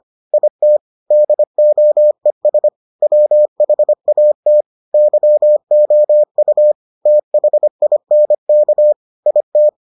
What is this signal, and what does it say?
Morse code, heard as a keyed tone, about 25 wpm, 600 Hz. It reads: IT DOES WHAT YOU THINK IT